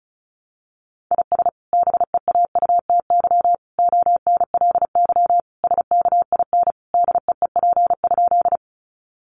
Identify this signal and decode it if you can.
Morse code, heard as a keyed tone, about 34 wpm, 700 Hz. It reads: IS BEAUTY ONLY SKIN DEEP?
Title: Sentences from Top 1300 Words 35wpm